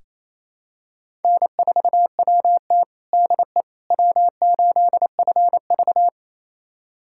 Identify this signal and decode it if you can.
Morse code, heard as a keyed tone, about 28 wpm, 700 Hz. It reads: N4WT DE W8FV
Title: USA Callsign DE Callsign - 4 Characters - 28wpm